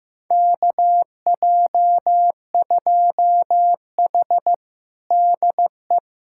Morse code, heard as a keyed tone, about 15 wpm, 700 Hz. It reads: KJ2H DE
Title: USA Callsign DE Callsign - 4 Characters - 15wpm